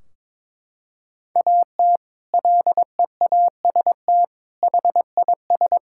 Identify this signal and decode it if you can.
Morse code, heard as a keyed tone, about 22 wpm, 700 Hz. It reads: AT LEAST HIS
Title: Sentences from Top 1600 Words 22wpm